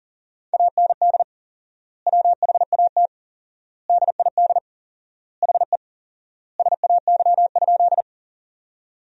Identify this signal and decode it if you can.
Morse code, heard as a keyed tone, about 40 wpm, 700 Hz. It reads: AND WHAT DID HE SAY?
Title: 4x Word-Spacing - Sentences from Top 200 Words - 40wpm